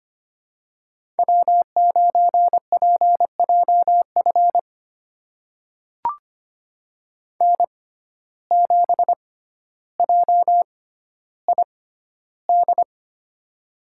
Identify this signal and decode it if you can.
Morse code, heard as a keyed tone, about 25 wpm, spaced as for 13 wpm, 700 Hz. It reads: W9PJF N7JID
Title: USA Call Signs - 5 Characters - 6x Character Spacing - 25wpm